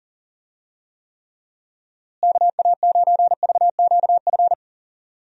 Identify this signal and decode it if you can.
Morse code, heard as a keyed tone, about 40 wpm, 700 Hz. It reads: KA9VQF
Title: USA Call Signs - 6 Characters - Repeated 2x - 40wpm